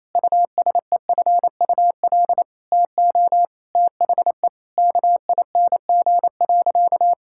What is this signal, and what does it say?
Morse code, heard as a keyed tone, about 28 wpm, 700 Hz. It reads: USEFUL TO THE KING.
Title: Sentences from Top 1000 Words - Review Part 6 - 28wpm